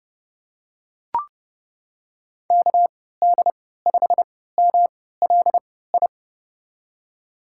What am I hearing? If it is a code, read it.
Morse code, heard as a keyed tone, about 30 wpm, spaced as for 17 wpm, 700 Hz. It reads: KD5MLI
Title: USA Call Signs - 6 Characters - 3x Character Spacing - 30wpm